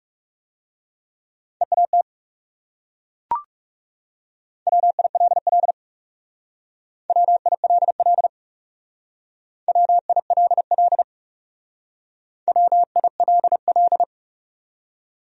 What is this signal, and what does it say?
Morse code, heard as a keyed tone, about 36 wpm, 700 Hz. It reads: EAT WILL WILL WILL WILL
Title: Speed-Racing Top 300 Words 30wpm